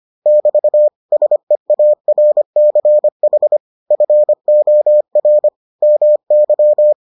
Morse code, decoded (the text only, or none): <BT> SEARCH FOR MY